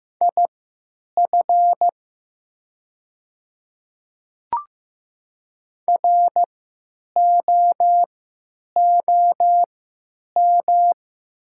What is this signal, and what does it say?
Morse code, heard as a keyed tone, about 15 wpm, spaced as for 9 wpm, 700 Hz. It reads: IF ROOM
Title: 3x Character Spacing - Top 300 Words - 15wpm